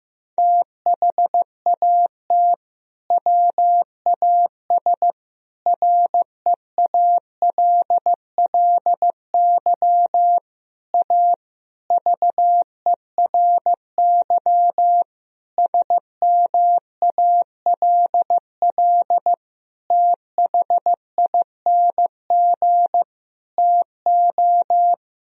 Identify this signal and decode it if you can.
Morse code, heard as a keyed tone, about 15 wpm, 700 Hz. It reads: THAT WAS REALLY A VERY SMALL THING TO
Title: Sentences from Top 1000 Words - Review Part 1 - 15wpm